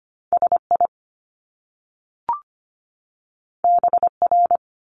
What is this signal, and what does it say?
Morse code, heard as a keyed tone, about 25 wpm, 700 Hz. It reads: SI BR
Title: Top Two Letter Combinations in Words 25wpm